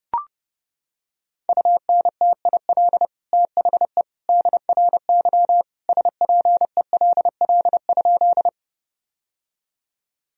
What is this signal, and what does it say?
Morse code, heard as a keyed tone, about 29 wpm, 700 Hz. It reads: UNTIL THE DRY SPELL?
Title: Sentences from Top 1000 Words - Review Part 3 - 30wpm